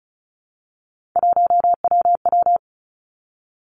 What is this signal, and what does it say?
Morse code, heard as a keyed tone, about 36 wpm, 700 Hz. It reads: 1WW
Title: Three Challenging Letter-Number Combinations 35wpm